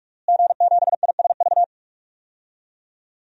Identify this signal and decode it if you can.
Morse code, heard as a keyed tone, about 46 wpm, 700 Hz. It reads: G7ISV
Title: DXCC-100 Call Signs - 5 Characters - Repeated 2x - 45wpm